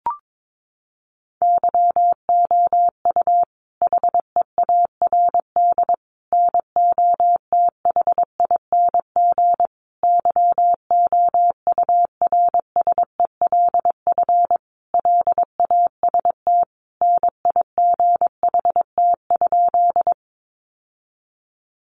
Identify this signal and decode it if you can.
Morse code, heard as a keyed tone, about 22 wpm, 700 Hz. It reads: YOU HEARD NOTHING YOURSELF LAST NIGHT?
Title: Sentences from Top 600 Words 22wpm